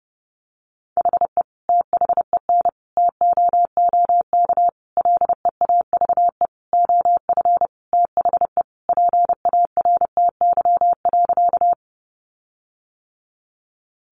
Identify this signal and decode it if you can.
Morse code, heard as a keyed tone, about 30 wpm, 700 Hz. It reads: HE THEN TOOK LEAVE OF THE PARTY.
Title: Sentences from Top 1400 Words 30wpm